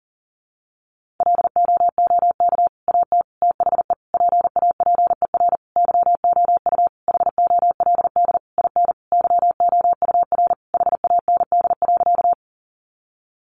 Morse code, heard as a keyed tone, about 40 wpm, 700 Hz. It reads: LOOK AT THE PAPER YOU HOLD IN YOUR HAND.
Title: Sentences from Top 400 Words 40wpm